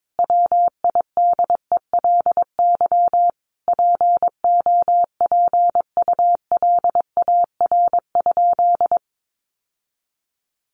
Morse code, heard as a keyed tone, about 22 wpm, 700 Hz. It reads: WIDELY POPULAR?
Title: Sentences from Top 1200 Words 22wpm